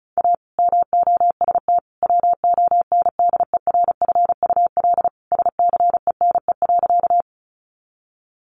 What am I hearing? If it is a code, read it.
Morse code, heard as a keyed tone, about 34 wpm, 700 Hz. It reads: A MOST WONDERFUL SCENE.
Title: Sentences from Top 1100 Words 35wpm